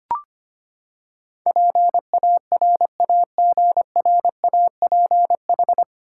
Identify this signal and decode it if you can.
Morse code, heard as a keyed tone, about 25 wpm, 700 Hz. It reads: PARAGRAPH
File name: Nine Letter Words - 25wpm